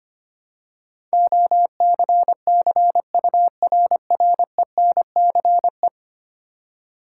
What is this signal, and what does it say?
Morse code, heard as a keyed tone, about 25 wpm, 700 Hz. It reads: OCCURRENCE
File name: Ten Letter Words - 25wpm